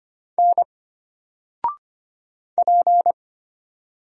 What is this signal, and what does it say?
Morse code, heard as a keyed tone, about 25 wpm, 700 Hz. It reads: N P
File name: Single-Letter-Number-25wpm